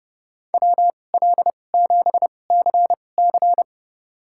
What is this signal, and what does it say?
Morse code, heard as a keyed tone, about 30 wpm, spaced as for 22 wpm, 700 Hz. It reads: WL7CC
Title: USA Call Signs - 5 Characters - 2x Character Spacing - 30wpm